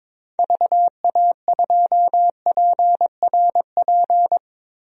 Morse code, heard as a keyed tone, about 22 wpm, 700 Hz. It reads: VA2PRP
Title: DXCC-100 Call Signs - 6 Characters - Repeated 2x - 22wpm